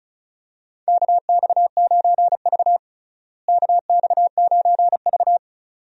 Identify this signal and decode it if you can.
Morse code, heard as a keyed tone, about 34 wpm, 700 Hz. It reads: KX9V KX9V
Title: USA Call Signs - 4 Characters - Repeated 2x - 35wpm